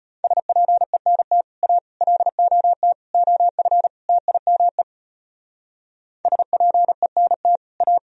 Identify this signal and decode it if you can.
Morse code, heard as a keyed tone, about 31 wpm, 700 Hz. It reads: SPENT A LOT OF TIME SPENT A